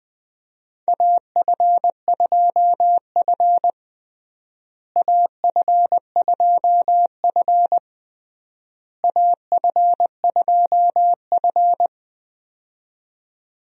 Morse code, heard as a keyed tone, about 20 wpm, 700 Hz. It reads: AF2F AF2F AF2F
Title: USA Call Signs - 4 Characters - Repeated 3x - 20wpm